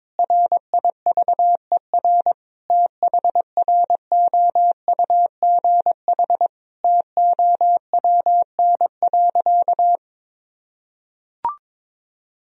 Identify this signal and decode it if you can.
Morse code, heard as a keyed tone, about 22 wpm, 700 Hz. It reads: RIVER THROUGH TOWN.
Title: Sentences from Top 1000 Words - Review Part 2 - 22wpm